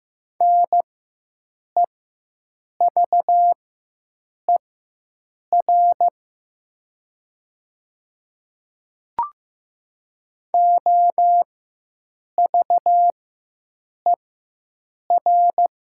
Morse code, heard as a keyed tone, about 15 wpm, spaced as for 7 wpm, 700 Hz. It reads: NEVER OVER
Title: 4x Character Spacing - Top 300 Words - 15wpm